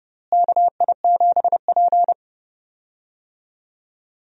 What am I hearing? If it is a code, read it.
Morse code, heard as a keyed tone, about 29 wpm, 700 Hz. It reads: KI7P
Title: USA Call Signs - 4 Characters - Repeated 2x - 30wpm